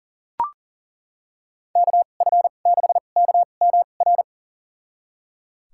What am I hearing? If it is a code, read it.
Morse code, heard as a keyed tone, about 39 wpm, spaced as for 28 wpm, 700 Hz. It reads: KF6KMR
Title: USA Call Signs - 6 Characters - 2x Character Spacing - 40wpm